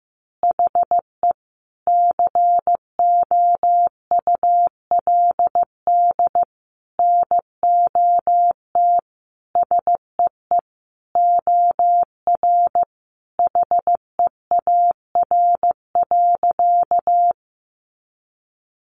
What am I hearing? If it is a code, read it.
Morse code, heard as a keyed tone, about 15 wpm, 700 Hz. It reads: HE COULD NOT SEE OR HEAR.
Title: Sentences from Top 600 Words 15wpm